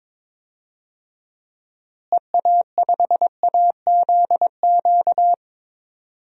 Morse code, heard as a keyed tone, about 22 wpm, 700 Hz. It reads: EA5AZQ
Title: DXCC-100 Call Signs - 6 Characters - Repeated 3x - 22wpm